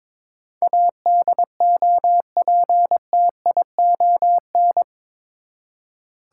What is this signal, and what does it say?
Morse code, heard as a keyed tone, about 22 wpm, 700 Hz. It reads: ADOPTION
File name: practice / 22wpm / Eight Letter Words - 22wpm